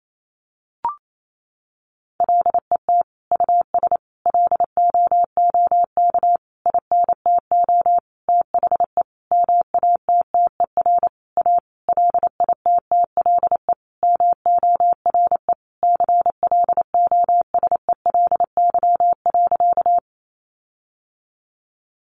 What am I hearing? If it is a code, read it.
Morse code, heard as a keyed tone, about 28 wpm, 700 Hz. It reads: LET US LOOK INTO THE MATTER A LITTLE MORE CLOSELY.
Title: Sentences from Top 1400 Words 28wpm